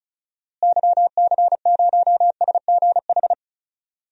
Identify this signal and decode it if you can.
Morse code, heard as a keyed tone, about 34 wpm, 700 Hz. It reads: YC0SGH
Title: DXCC-100 Call Signs - 6 Characters - Repeated 3x - 35wpm